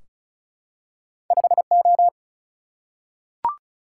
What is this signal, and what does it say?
Morse code, heard as a keyed tone, about 35 wpm, 700 Hz. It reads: 5O
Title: Single-Letter Single-Number 35wpm